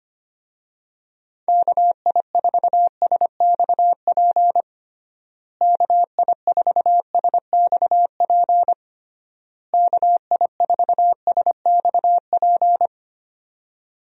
Morse code, heard as a keyed tone, about 25 wpm, 700 Hz. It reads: KI4SXP KI4SXP KI4SXP
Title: USA Call Signs - 6 Characters - Repeated 3x - 25wpm